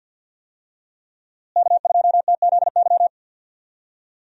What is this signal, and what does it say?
Morse code, heard as a keyed tone, about 51 wpm, 700 Hz. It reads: K2TZY
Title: USA Call Signs - 5 Characters - Repeated 3x - 50wpm